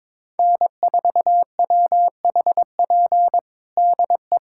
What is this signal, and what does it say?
Morse code, heard as a keyed tone, about 22 wpm, 700 Hz. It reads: N4WHP DE
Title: USA Callsign DE Callsign - 5 Characters - 22wpm